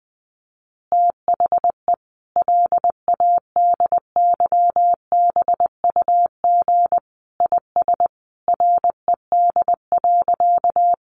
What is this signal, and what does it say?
Morse code, heard as a keyed tone, about 20 wpm, 700 Hz. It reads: THE LADYBUG IS RED.
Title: Additional Easy Phrases 20wpm